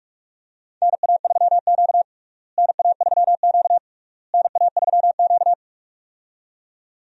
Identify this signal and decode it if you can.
Morse code, heard as a keyed tone, about 46 wpm, 700 Hz. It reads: NA3Q NA3Q NA3Q
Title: USA Call Signs - 4 Characters - Repeated 3x - 45wpm